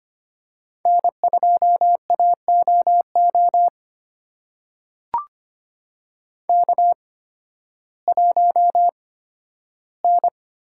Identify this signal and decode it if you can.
Morse code, heard as a keyed tone, about 25 wpm, spaced as for 13 wpm, 700 Hz. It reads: N2AOO K1N
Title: USA Call Signs - 5 Characters - 8x Character Spacing - 25wpm